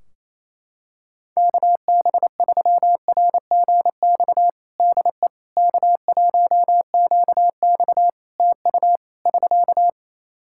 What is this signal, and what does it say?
Morse code, heard as a keyed tone, about 27 wpm, 700 Hz. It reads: KB3RGX DE K1QX TU <SK>
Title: Practice QSO - Exchange 4 - Ending - 28wpm